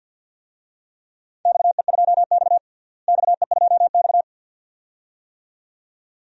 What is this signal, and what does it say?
Morse code, heard as a keyed tone, about 53 wpm, 700 Hz. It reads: XE2X XE2X
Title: DXCC-100 Call Signs - 4 Characters - Repeated 2x - 50wpm